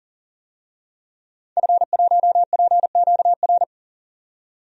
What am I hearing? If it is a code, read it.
Morse code, heard as a keyed tone, about 40 wpm, 700 Hz. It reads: F1PQR
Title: DXCC-100 Call Signs - 5 Characters - 40wpm